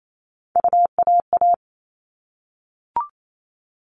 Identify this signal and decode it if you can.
Morse code, heard as a keyed tone, about 28 wpm, 700 Hz. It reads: UAA
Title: All Cut Numbers - 3-Digit 28wpm